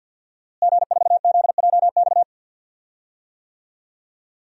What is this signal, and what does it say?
Morse code, heard as a keyed tone, about 50 wpm, 700 Hz. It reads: G4ZJX
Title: DXCC-100 Call Signs - 5 Characters - Repeated 3x - 50wpm